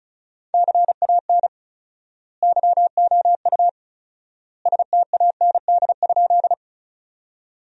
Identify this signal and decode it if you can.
Morse code, heard as a keyed tone, about 36 wpm, 700 Hz. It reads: CAN YOU STAND?
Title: 4x Word-Spacing - Sentences from Top 300 Words - 35wpm